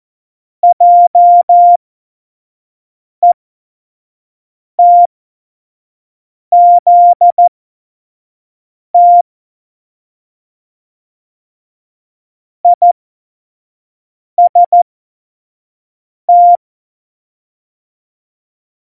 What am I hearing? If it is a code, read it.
Morse code, heard as a keyed tone, about 14 wpm, spaced as for 5 wpm, 700 Hz. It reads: JETZT IST
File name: Morsekurs / Witze / Witz 12. September 17, Tempo 25